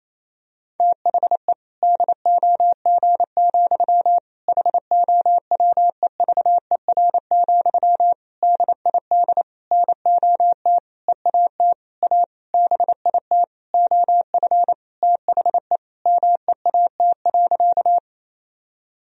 Morse code, heard as a keyed tone, about 27 wpm, 700 Hz. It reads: THE DOG, HOWEVER, DID NOT EAT A BIT OF THE MEAT.
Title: Sentences from Top 1000 Words - Review Part 3 - 28wpm